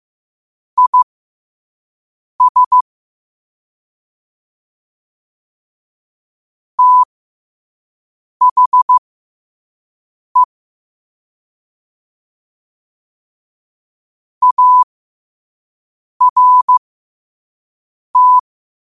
Morse code, heard as a keyed tone, about 15 wpm, spaced as for 5 wpm, 1000 Hz. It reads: IS THE ART
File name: qotdfiles / 5wpm